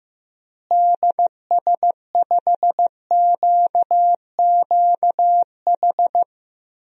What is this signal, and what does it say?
Morse code, heard as a keyed tone, about 15 wpm, 700 Hz. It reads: DS5QQH
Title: DXCC-100 Call Signs - 6 Characters - Repeated 3x - 15wpm